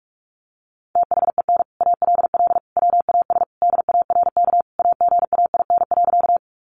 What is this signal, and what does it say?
Morse code, heard as a keyed tone, about 44 wpm, 700 Hz. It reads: THEN ALL WAS DARK AGAIN.
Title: Sentences from Top 500 Words 45wpm